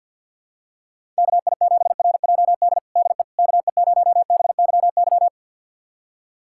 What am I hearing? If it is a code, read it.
Morse code, heard as a keyed tone, about 52 wpm, 700 Hz. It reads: KI7RJD DE KE0BYY
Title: USA Callsign DE Callsign - 6 Characters - 50wpm